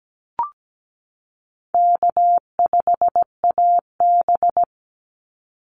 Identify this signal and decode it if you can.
Morse code, heard as a keyed tone, about 17 wpm, 700 Hz. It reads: K5AB
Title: Call Signs Repeated 3x - 17wpm